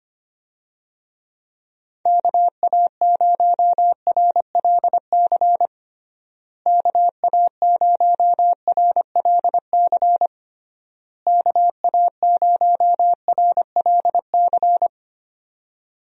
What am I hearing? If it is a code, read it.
Morse code, heard as a keyed tone, about 25 wpm, 700 Hz. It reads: KA0RLC KA0RLC KA0RLC
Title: USA Call Signs - 6 Characters - Repeated 3x - 25wpm